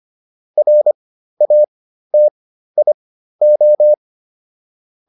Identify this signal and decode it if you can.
Morse code, heard as a keyed tone, about 25 wpm, spaced as for 13 wpm, 600 Hz. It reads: RATIO